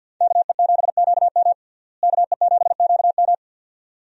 Morse code, heard as a keyed tone, about 50 wpm, 700 Hz. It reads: KE7QM KE7QM
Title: USA Call Signs - 5 Characters - Repeated 2x - 50wpm